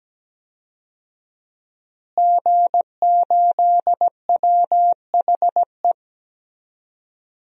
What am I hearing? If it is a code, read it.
Morse code, heard as a keyed tone, about 17 wpm, 700 Hz. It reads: G8WHE
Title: DXCC-100 Call Signs - 5 Characters - Repeated 2x - 17wpm